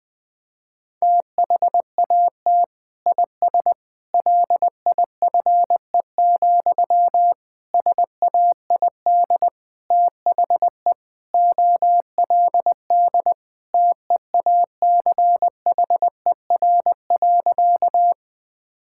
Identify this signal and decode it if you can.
Morse code, heard as a keyed tone, about 20 wpm, 700 Hz. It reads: THAT IS LIFE, SAID THE OLD TEACHER.
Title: Sentences from Top 1000 Words - Review Part 6 - 20wpm